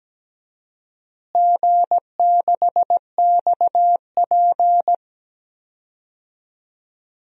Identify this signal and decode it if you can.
Morse code, heard as a keyed tone, about 17 wpm, 700 Hz. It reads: G6XP
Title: DXCC-100 Call Signs - 4 Characters - Repeated 3x - 17wpm